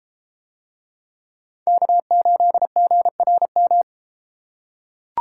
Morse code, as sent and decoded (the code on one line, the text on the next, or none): -.- ---.. --. .-. --
K8GRM